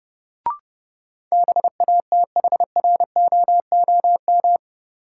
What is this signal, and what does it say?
Morse code, heard as a keyed tone, about 30 wpm, 700 Hz. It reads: BATHROOM
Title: Sets of 5 Words Encore - New Word Drill - 30wpm